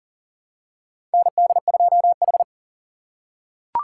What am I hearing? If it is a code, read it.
Morse code, heard as a keyed tone, about 39 wpm, 700 Hz. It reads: ND2H
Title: USA Call Signs - 4 Characters - 6x Character Spacing - 40wpm